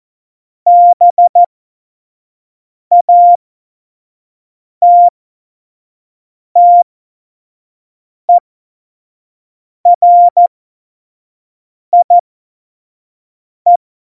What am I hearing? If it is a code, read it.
Morse code, heard as a keyed tone, about 14 wpm, spaced as for 5 wpm, 700 Hz. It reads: BATTERIE
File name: 25 F70